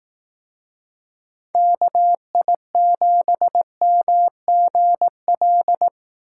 Morse code, heard as a keyed tone, about 18 wpm, 700 Hz. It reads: KI7MGL
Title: Speed Racing - USA 2x3 Call Signs 15wpm